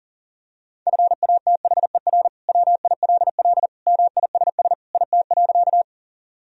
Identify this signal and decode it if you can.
Morse code, heard as a keyed tone, about 40 wpm, 700 Hz. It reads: FATHER WILL MISS IT.
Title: Sentences from Top 1000 Words - Review Part 3 - 40wpm